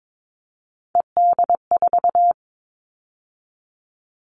Morse code, heard as a keyed tone, about 22 wpm, 700 Hz. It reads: ED4
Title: All Cut Numbers - 3-Digit 22wpm